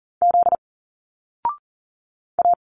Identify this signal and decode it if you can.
Morse code, heard as a keyed tone, about 40 wpm, 700 Hz. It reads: Z A
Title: Single Letters 40wpm